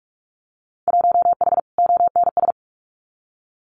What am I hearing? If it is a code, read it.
Morse code, heard as a keyed tone, about 44 wpm, 700 Hz. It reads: 1H ONS
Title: Winter Field Day Exchanges 45wpm